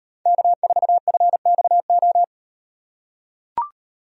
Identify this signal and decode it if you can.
Morse code, heard as a keyed tone, about 38 wpm, 700 Hz. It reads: K4FXO